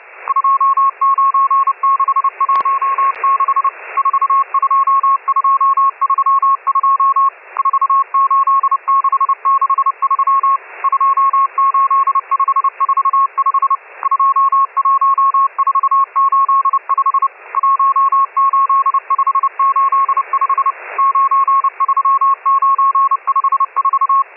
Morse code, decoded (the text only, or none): 29616 42232 48663 28545 21485 18585 93954